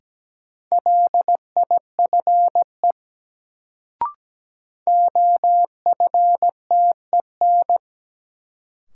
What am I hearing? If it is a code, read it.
Morse code, heard as a keyed tone, about 17 wpm, 700 Hz. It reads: LIFE OFTEN
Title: Sentences from Top 300 Words - New Word Drill - 17wpm